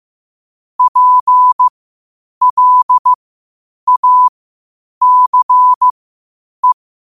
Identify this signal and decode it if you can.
Morse code, heard as a keyed tone, about 15 wpm, spaced as for 8 wpm, 1000 Hz. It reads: PLACE
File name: qotdfiles / 8wpm